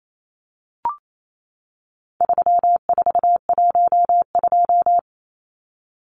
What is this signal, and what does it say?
Morse code, heard as a keyed tone, about 28 wpm, 700 Hz. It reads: 3412
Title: Numbers - 4-Digit 28wpm